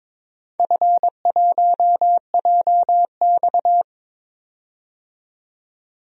Morse code, heard as a keyed tone, about 22 wpm, 700 Hz. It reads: F1JX
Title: DXCC-100 Call Signs - 4 Characters - 22wpm